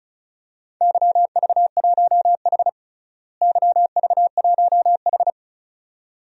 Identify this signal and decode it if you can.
Morse code, heard as a keyed tone, about 36 wpm, 700 Hz. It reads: YV1H YV1H
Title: DXCC-100 Call Signs - 4 Characters - Repeated 2x - 35wpm